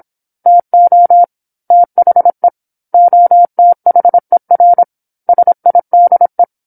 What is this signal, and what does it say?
Morse code, heard as a keyed tone, about 27 wpm, 700 Hz. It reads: TO THE OTHER SIDE
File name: 08-Chapter-08 26 WPM